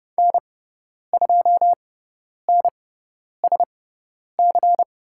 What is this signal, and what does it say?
Morse code, heard as a keyed tone, about 30 wpm, spaced as for 10 wpm, 700 Hz. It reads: N2NSC
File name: Lesson 17g - Letter u - Call Signs 30wpm